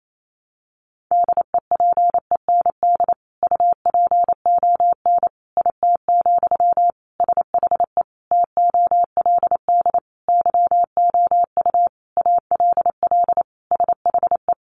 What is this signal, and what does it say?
Morse code, heard as a keyed tone, about 28 wpm, 700 Hz. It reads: DEPEND UPON IT, SHE TOLD YOU ALL SHE
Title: Sentences from Top 1300 Words 28wpm